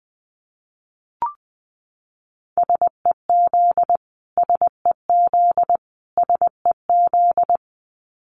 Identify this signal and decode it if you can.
Morse code, heard as a keyed tone, about 20 wpm, 700 Hz. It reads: SEZ SEZ SEZ
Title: Abbreviations Repeated 3x - 20wpm